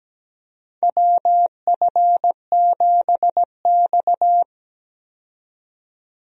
Morse code, heard as a keyed tone, about 17 wpm, 700 Hz. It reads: WF7X
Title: USA Call Signs - 4 Characters - 17wpm